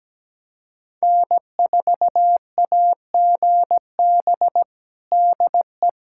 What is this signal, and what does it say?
Morse code, heard as a keyed tone, about 17 wpm, 700 Hz. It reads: N4AGB DE
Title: USA Callsign DE Callsign - 5 Characters - 17wpm